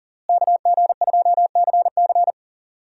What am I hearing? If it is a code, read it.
Morse code, heard as a keyed tone, about 40 wpm, 700 Hz. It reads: KG2CC
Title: USA Call Signs - 5 Characters - Repeated 2x - 40wpm